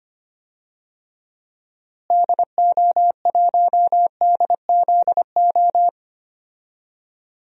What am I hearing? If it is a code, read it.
Morse code, heard as a keyed tone, about 25 wpm, 700 Hz. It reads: DO1DZO
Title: DXCC-100 Call Signs - 6 Characters - Repeated 2x - 25wpm